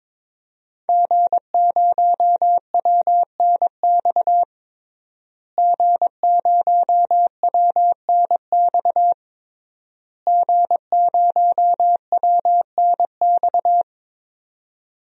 Morse code, heard as a keyed tone, about 21 wpm, 700 Hz. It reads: G0WNX G0WNX G0WNX
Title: DXCC-100 Call Signs - 5 Characters - Repeated 3x - 22wpm